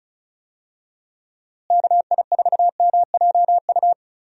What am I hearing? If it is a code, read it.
Morse code, heard as a keyed tone, about 34 wpm, 700 Hz. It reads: KI4MJU
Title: USA Call Signs - 6 Characters - Repeated 3x - 35wpm